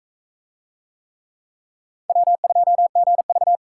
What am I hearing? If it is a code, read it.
Morse code, heard as a keyed tone, about 36 wpm, 700 Hz. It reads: W2GV